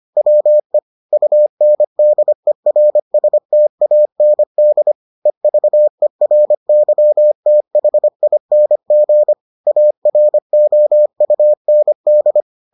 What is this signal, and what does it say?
Morse code, heard as a keyed tone, about 25 wpm, 600 Hz. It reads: WE UNDERSTAND EVERYTHING AROUND